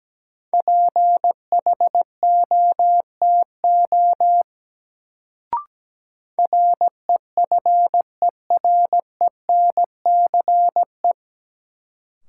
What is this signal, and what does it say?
Morse code, heard as a keyed tone, about 17 wpm, 700 Hz. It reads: PHOTO REFERENCE
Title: Sentences from Top 1200 Words - New Word Drill - 17wpm